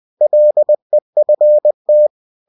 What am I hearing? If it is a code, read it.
Morse code, heard as a keyed tone, about 20 wpm, 600 Hz. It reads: LEFT
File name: cw-books / winnie-the-pooh / winnie-20wpm-page-0005